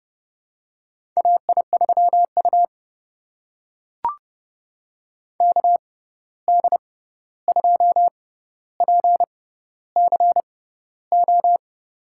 Morse code, heard as a keyed tone, about 30 wpm, spaced as for 14 wpm, 700 Hz. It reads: AI3U KD2PCO
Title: USA Call Signs - 6x Character Spacing - 30wpm